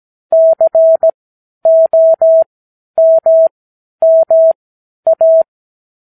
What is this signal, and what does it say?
Morse code, heard as a keyed tone, about 17 wpm, spaced as for 10 wpm, 650 Hz. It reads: COMMA
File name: k7qo / Lesson 58 Comma